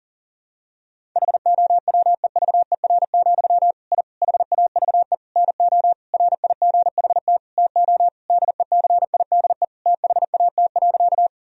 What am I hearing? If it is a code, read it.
Morse code, heard as a keyed tone, about 40 wpm, 700 Hz. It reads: HOWEVER, I HAVE NO RIGHT TO DECIDE THAT.
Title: Sentences from Top 1000 Words - Review Part 3 - 40wpm